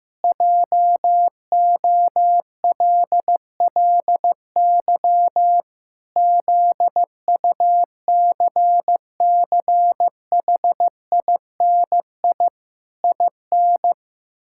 Morse code, heard as a keyed tone, about 15 wpm, 700 Hz. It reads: JOLLY ZUCCHINI IN
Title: Pangram Challenge v2 - 15wpm